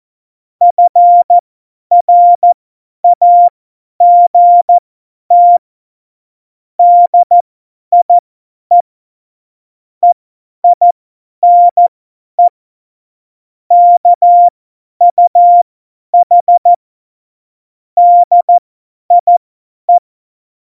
Morse code, heard as a keyed tone, about 14 wpm, spaced as for 10 wpm, 700 Hz. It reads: FRAGT DIE EINE KUH DIE
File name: Morsekurs / Witze / Witz 29. Mai 2018, 50 F70